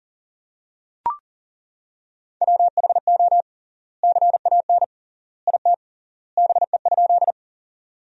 Morse code, heard as a keyed tone, about 40 wpm, 700 Hz. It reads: WHO CAN IT BE?
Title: 3x Word-Spacing - Sentences from Top 100 Words - 40wpm